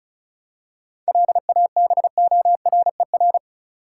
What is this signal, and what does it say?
Morse code, heard as a keyed tone, about 34 wpm, 700 Hz. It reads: LABORER
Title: Practice QSO - Exchange 3 - More Info - 35wpm